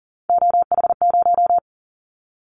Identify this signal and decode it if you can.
Morse code, heard as a keyed tone, about 40 wpm, 700 Hz. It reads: OH0
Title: Callsign Prefixes for Top 100 DXCC - 40wpm